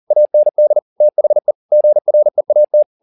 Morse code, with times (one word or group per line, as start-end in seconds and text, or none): 0.1-0.8 AND
1.0-1.5 THE
1.7-2.8 GREAT